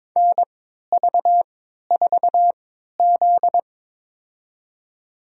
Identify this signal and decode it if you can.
Morse code, heard as a keyed tone, about 23 wpm, spaced as for 13 wpm, 700 Hz. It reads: NV4Z
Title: USA Call Signs - 3x Character Spacing - 22wpm